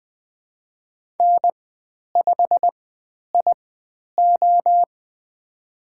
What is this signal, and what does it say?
Morse code, heard as a keyed tone, about 20 wpm, spaced as for 10 wpm, 700 Hz. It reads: N5IO
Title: 102 Lesson 17g - Letter u - Call Signs 20wpm